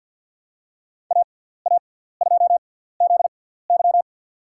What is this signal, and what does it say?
Morse code, heard as a keyed tone, about 53 wpm, spaced as for 17 wpm, 700 Hz. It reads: AA2ZY
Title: USA Call Signs - 5 Characters - 6x Character Spacing - 50wpm